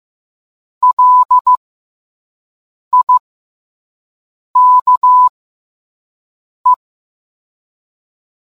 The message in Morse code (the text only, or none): LIKE